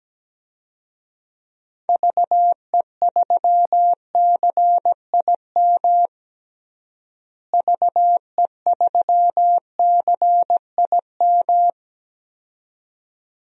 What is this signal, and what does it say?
Morse code, heard as a keyed tone, about 17 wpm, 700 Hz. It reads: VE3CIM VE3CIM
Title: DXCC-100 Call Signs - 6 Characters - Repeated 2x - 17wpm